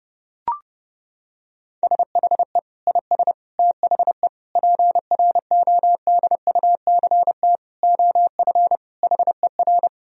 Morse code, heard as a keyed tone, about 32 wpm, 700 Hz. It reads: SHE IS THE PRODUCT OF HER
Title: Sentences from Top 1000 Words - Review Part 2 - 30wpm